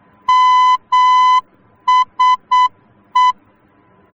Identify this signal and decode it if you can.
Morse code, heard as a keyed tone, about 8 wpm, 1010 Hz. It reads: MSE